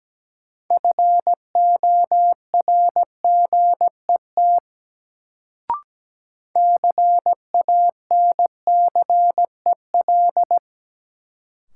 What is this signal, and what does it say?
Morse code, heard as a keyed tone, about 17 wpm, 700 Hz. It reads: FORGET CANCEL
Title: Sentences from Top 1200 Words - New Word Drill Alt - 17wpm